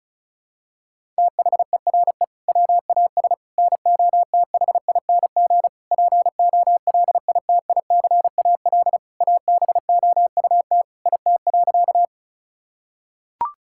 Morse code, heard as a keyed tone, about 36 wpm, 700 Hz. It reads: THERE WAS NOTHING POLITICAL ABOUT IT.
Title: Sentences from Top 1000 Words - Review Part 5 - 35wpm